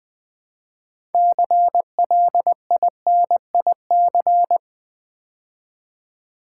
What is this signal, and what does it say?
Morse code, heard as a keyed tone, about 20 wpm, 700 Hz. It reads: CLINIC
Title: Build A Word - 3 Letters At A Time - 6-Letter Words - 20wpm